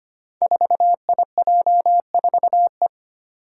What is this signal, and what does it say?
Morse code, heard as a keyed tone, about 25 wpm, 700 Hz. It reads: 4IJ4E
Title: Five Challenging Letter-Number Combinations 25wpm